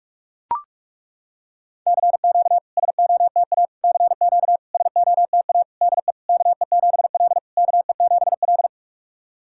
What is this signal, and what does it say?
Morse code, heard as a keyed tone, about 41 wpm, 690 Hz. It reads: CQ SOTA CQ SOTA DE KE7L KE7L
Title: SOTA QSO Practice 45wpm